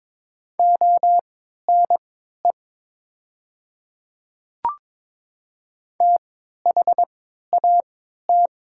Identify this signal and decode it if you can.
Morse code, heard as a keyed tone, about 22 wpm, spaced as for 12 wpm, 700 Hz. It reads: ONE THAT
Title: 3x Character Spacing - Top 100 Words - 22wpm